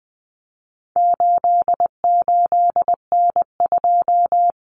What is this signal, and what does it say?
Morse code, heard as a keyed tone, about 20 wpm, 700 Hz. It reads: 88N2
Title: Common Cut Numbers - 4-Digit 20wpm